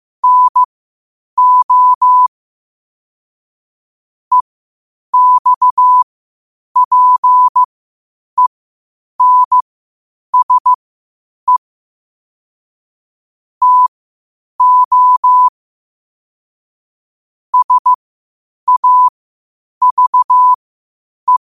Morse code, heard as a keyed tone, about 15 wpm, spaced as for 8 wpm, 1000 Hz. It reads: NO EXPENSE TO SAVE
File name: qotdfiles / 8wpm